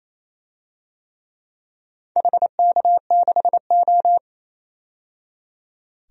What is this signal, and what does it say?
Morse code, heard as a keyed tone, about 28 wpm, 700 Hz. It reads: HK6O
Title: DXCC-100 Call Signs - 4 Characters - 28wpm